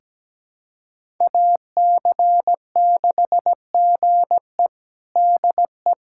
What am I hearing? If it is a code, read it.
Morse code, heard as a keyed tone, about 17 wpm, 700 Hz. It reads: AC6GE DE
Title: USA Callsign DE Callsign - 5 Characters - 17wpm